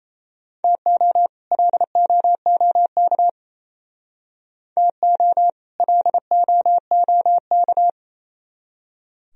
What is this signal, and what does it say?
Morse code, heard as a keyed tone, about 37 wpm, 700 Hz. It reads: TO LOOK TO LOOK